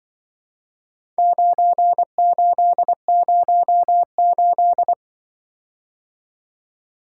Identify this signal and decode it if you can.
Morse code, heard as a keyed tone, about 24 wpm, 700 Hz. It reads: 9808
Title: Speed-Racing - Four Letter-Number Combinations - 20wpm